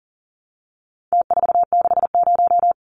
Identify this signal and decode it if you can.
Morse code, heard as a keyed tone, about 40 wpm, 700 Hz. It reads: T460
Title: Numbers - Contest Style - 4-Digit 40wpm